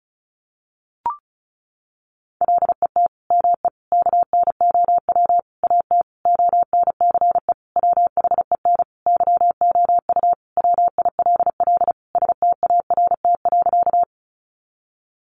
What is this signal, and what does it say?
Morse code, heard as a keyed tone, about 34 wpm, 700 Hz. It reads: LET ME KNOW AT ONCE WHEN YOU WILL START.
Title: Sentences from Top 300 Words 35wpm